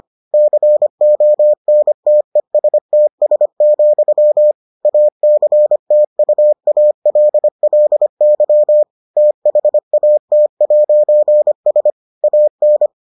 Morse code, heard as a keyed tone, about 25 wpm, 600 Hz. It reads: CONTESTS, ACTUALLY THAT'S AN